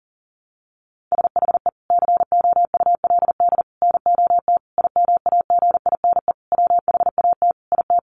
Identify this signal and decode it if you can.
Morse code, heard as a keyed tone, about 40 wpm, 700 Hz. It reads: SHE COULD NOT IMAGINE WHAT IT
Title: Sentences from Top 1400 Words 40wpm